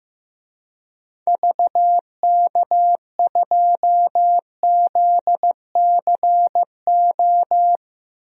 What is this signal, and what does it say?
Morse code, heard as a keyed tone, about 15 wpm, 700 Hz. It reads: VK2ZCO
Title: DXCC-100 Call Signs - 6 Characters - 15wpm